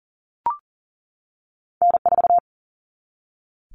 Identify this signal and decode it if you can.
Morse code, heard as a keyed tone, about 40 wpm, 700 Hz. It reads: N4
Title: Common Cut Numbers - 2-Digit 40wpm